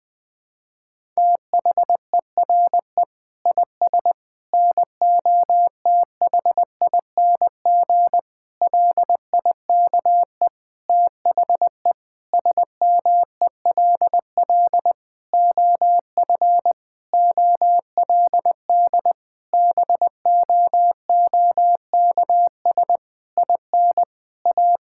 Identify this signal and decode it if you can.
Morse code, heard as a keyed tone, about 21 wpm, 700 Hz. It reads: THERE IS NOTHING LIKE THE SMELL OF OLD BOOKS IN A
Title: Difficult Sentences with Hints - 20wpm